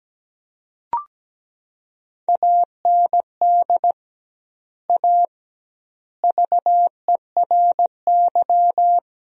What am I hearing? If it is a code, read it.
Morse code, heard as a keyed tone, about 17 wpm, 700 Hz. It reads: AND A VERY
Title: 2x Word-Spacing - Sentences from Top 200 Words - 17wpm